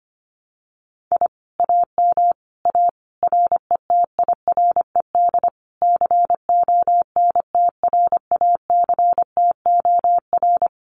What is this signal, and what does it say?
Morse code, heard as a keyed tone, about 25 wpm, 700 Hz. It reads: I AM A RETIRED CONTRACTOR
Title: Common QSO Phrases Encore 25wpm